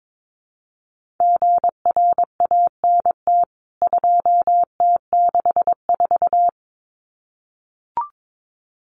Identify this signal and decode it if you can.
Morse code, heard as a keyed tone, about 22 wpm, 700 Hz. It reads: GRANT 2T64
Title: CWT Exchange 22wpm